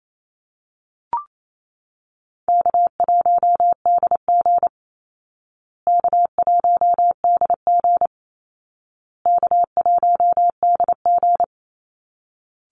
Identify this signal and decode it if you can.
Morse code, heard as a keyed tone, about 28 wpm, 700 Hz. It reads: K1DG K1DG K1DG
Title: Call Signs Repeated 2x - 28wpm